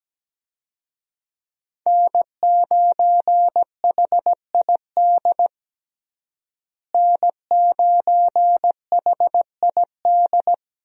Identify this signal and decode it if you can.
Morse code, heard as a keyed tone, about 17 wpm, 700 Hz. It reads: N9HID N9HID
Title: USA Call Signs - 5 Characters - Repeated 2x - 17wpm